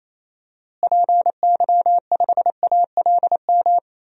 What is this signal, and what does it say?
Morse code, heard as a keyed tone, about 28 wpm, 700 Hz. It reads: PY5ALM
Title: DXCC-100 Call Signs - 6 Characters - Repeated 3x - 28wpm